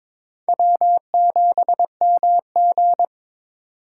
Morse code, heard as a keyed tone, about 25 wpm, 700 Hz. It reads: W7MG
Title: Speed Racing - USA 1x2 Call Signs 22wpm